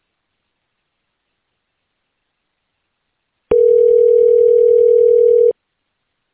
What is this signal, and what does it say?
Ringback tone (North America), 440 + 480 Hz.